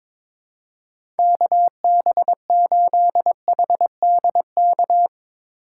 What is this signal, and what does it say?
Morse code, heard as a keyed tone, about 23 wpm, 700 Hz. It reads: KB8HDK
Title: USA Call Signs - 6 Characters - 8x Character Spacing - 22wpm